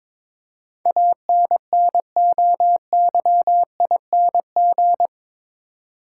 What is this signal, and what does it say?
Morse code, heard as a keyed tone, about 22 wpm, 700 Hz. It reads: ANNOYING
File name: practice / 22wpm / Eight Letter Words - 22wpm